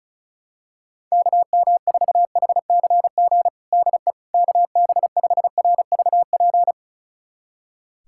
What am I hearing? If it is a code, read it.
Morse code, heard as a keyed tone, about 36 wpm, 700 Hz. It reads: KM4HCG DE KB5RVP
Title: USA Callsign DE Callsign 35wpm